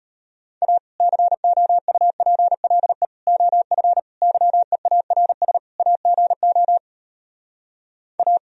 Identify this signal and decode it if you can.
Morse code, heard as a keyed tone, about 33 wpm, 700 Hz. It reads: A COUPLE OF YEARS AGO A